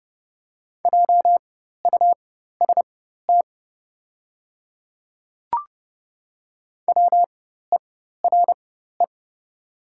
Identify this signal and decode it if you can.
Morse code, heard as a keyed tone, about 30 wpm, spaced as for 14 wpm, 700 Hz. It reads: JUST WERE
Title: 4x Character Spacing - Top 200 Words - 30wpm